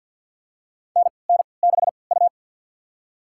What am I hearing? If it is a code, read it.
Morse code, heard as a keyed tone, about 52 wpm, spaced as for 29 wpm, 700 Hz. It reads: NN6U
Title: USA Call Signs - 4 Characters - 3x Character Spacing - 50wpm